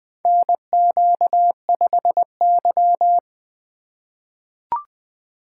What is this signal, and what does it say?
Morse code, heard as a keyed tone, about 20 wpm, 700 Hz. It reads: NQ5Y
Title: USA Call Signs - 4 Characters - 3x Character Spacing - 20wpm